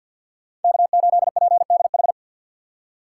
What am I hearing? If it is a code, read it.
Morse code, heard as a keyed tone, about 52 wpm, 690 Hz. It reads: K8PDH
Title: USA Call Signs - 5 Characters - 3x Character Spacing - 50wpm